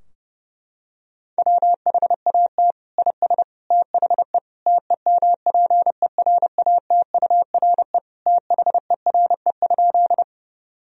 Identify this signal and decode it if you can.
Morse code, heard as a keyed tone, about 30 wpm, 700 Hz. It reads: WHAT IS THE TEMPERATURE THERE?
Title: Sentences from Top 1000 Words - Review Part 4 - 30wpm